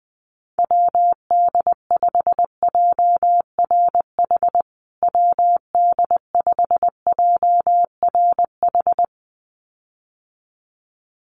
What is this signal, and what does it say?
Morse code, heard as a keyed tone, about 20 wpm, 700 Hz. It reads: WD5JRH WD5JRH
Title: USA Call Signs Repeated 2x 20wpm